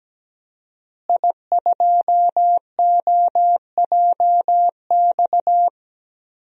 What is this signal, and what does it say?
Morse code, heard as a keyed tone, about 17 wpm, 700 Hz. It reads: I2OJX
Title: DXCC-100 Call Signs - 5 Characters - Repeated 3x - 17wpm